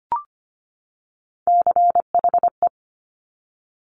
Morse code, heard as a keyed tone, about 25 wpm, 700 Hz. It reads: CHE
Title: Top Three Letter Combinations in Words 25wpm